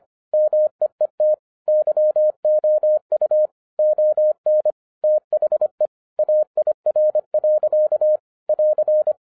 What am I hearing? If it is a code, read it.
Morse code, heard as a keyed tone, about 25 wpm, 610 Hz. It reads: MEET YOU ON THE AIR. <AR>